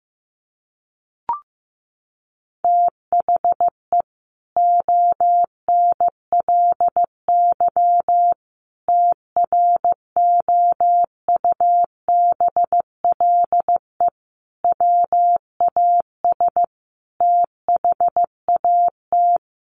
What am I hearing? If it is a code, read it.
Morse code, heard as a keyed tone, about 15 wpm, 700 Hz. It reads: THE ONLY TROUBLE WAS THAT
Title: Sentences from Top 1200 Words 15wpm